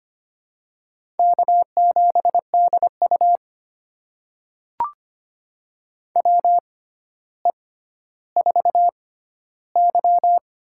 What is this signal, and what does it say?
Morse code, heard as a keyed tone, about 25 wpm, spaced as for 13 wpm, 700 Hz. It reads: K7DU WE4Y
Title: USA Call Signs - 6x Character Spacing - 25wpm